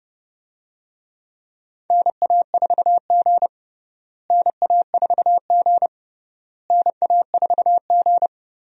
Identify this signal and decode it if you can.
Morse code, heard as a keyed tone, about 29 wpm, 700 Hz. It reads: NA4G NA4G NA4G
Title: USA Call Signs - 4 Characters - Repeated 3x - 30wpm